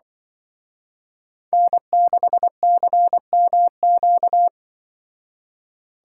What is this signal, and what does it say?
Morse code, heard as a keyed tone, about 24 wpm, 700 Hz. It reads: N6CMQ
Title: Speed Racing - USA 1x3 Call Signs 20wpm